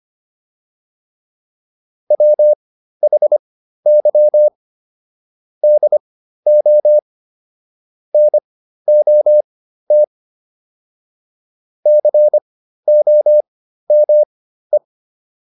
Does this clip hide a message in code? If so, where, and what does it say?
Yes: Morse code from 2.1 to 14.8 s, heard as a keyed tone, about 25 wpm, spaced as for 13 wpm, 600 Hz. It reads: WHY DO NOT COME